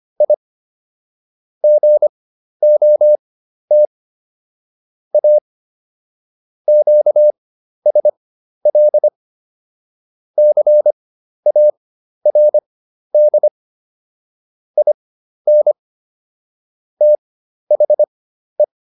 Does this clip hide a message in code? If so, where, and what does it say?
Yes: Morse code from 0.2 to 18.6 s, heard as a keyed tone, about 25 wpm, spaced as for 12 wpm, 600 Hz. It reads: I GOT A QSL CARD IN THE